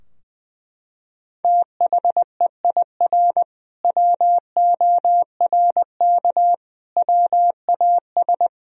Morse code, heard as a keyed tone, about 20 wpm, 700 Hz. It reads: THEIR WORK WAS
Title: Sentences from Top 1000 Words - Review Part 3 - 20wpm